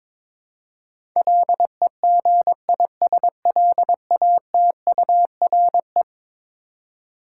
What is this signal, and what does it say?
Morse code, heard as a keyed tone, about 22 wpm, 700 Hz. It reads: LEGISLATURE
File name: practice / 22wpm / Eleven Letter Words - 22wpm